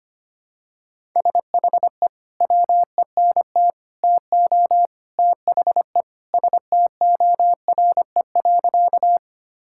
Morse code, heard as a keyed tone, about 25 wpm, 700 Hz. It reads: SHE WENT TO THE STORE.
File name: practice / 25wpm / Sentences with 3-Word Phrases - 25wpm